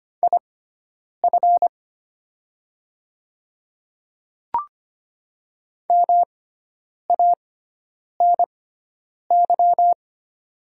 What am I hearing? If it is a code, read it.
Morse code, heard as a keyed tone, about 25 wpm, spaced as for 9 wpm, 700 Hz. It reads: IF MANY
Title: 6x Character Spacing - Top 200 Words - 25wpm